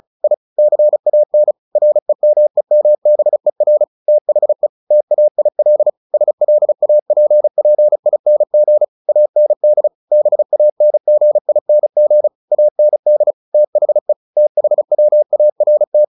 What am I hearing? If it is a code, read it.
Morse code, heard as a keyed tone, about 34 wpm, 600 Hz. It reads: I CAN REMEMBER THE TAIL SLAPPING AND BANGING AND THE THWART
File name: the old man and the sea / the old man and the sea 1 35wpm